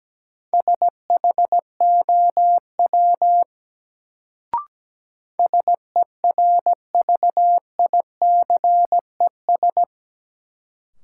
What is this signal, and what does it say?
Morse code, heard as a keyed tone, about 17 wpm, 700 Hz. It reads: SHOW SERVICES
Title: Sets of 5 Words - New Word Drill - 17wpm